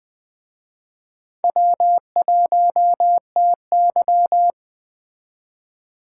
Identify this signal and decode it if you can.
Morse code, heard as a keyed tone, about 20 wpm, 700 Hz. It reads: W1TY
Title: Speed Racing - USA 1x2 Call Signs 17wpm